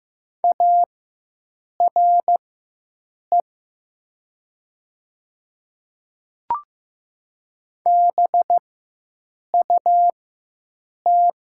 Morse code, heard as a keyed tone, about 15 wpm, spaced as for 7 wpm, 700 Hz. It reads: ARE BUT
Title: 4x Character Spacing - Top 100 Words - 15wpm